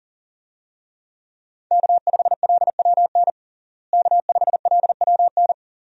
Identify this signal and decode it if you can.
Morse code, heard as a keyed tone, about 40 wpm, 700 Hz. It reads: K5LWN K5LWN
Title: USA Call Signs - 5 Characters - Repeated 3x - 40wpm